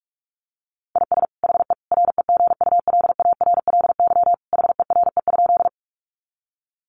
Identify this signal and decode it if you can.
Morse code, heard as a keyed tone, about 44 wpm, 700 Hz. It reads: IS HE REGULARLY HERE?
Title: Sentences from Top 1400 Words 45wpm